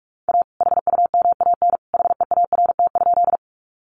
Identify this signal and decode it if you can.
Morse code, heard as a keyed tone, about 44 wpm, 700 Hz. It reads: A HUMAN HEART?
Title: Sentences from Top 1200 Words 45wpm